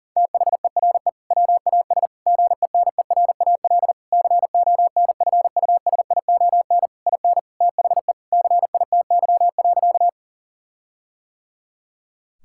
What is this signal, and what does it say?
Morse code, heard as a keyed tone, about 40 wpm, 700 Hz. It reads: THERE WAS GENERAL CONFUSION IN THE CITY.
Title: Sentences from Top 1600 Words 40wpm